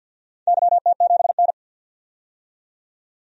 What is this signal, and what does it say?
Morse code, heard as a keyed tone, about 53 wpm, 700 Hz. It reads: YT7N
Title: DXCC-100 Call Signs - 4 Characters - 50wpm